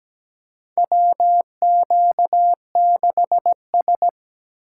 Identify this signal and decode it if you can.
Morse code, heard as a keyed tone, about 17 wpm, 700 Hz. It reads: WQ6S
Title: USA Call Signs - 2x Character Spacing - 17wpm